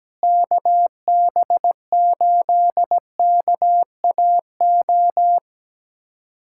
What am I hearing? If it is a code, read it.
Morse code, heard as a keyed tone, about 17 wpm, 700 Hz. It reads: KB8KAO
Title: USA Callsign DE Callsign 17wpm